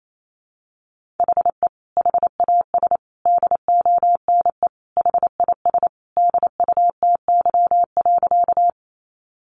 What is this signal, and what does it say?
Morse code, heard as a keyed tone, about 28 wpm, 700 Hz. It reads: HE HAS DONE HIS DUTY.
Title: Sentences from Top 1500 Words 28wpm